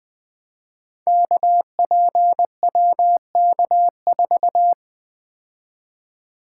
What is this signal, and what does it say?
Morse code, heard as a keyed tone, about 20 wpm, 700 Hz. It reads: KPWK4
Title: Code Groups - 5 Letter:Numbers - 20wpm